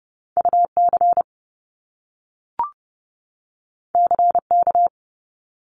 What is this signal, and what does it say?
Morse code, heard as a keyed tone, about 30 wpm, 700 Hz. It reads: UC CK
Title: Top Two Letter Combinations in Words 30wpm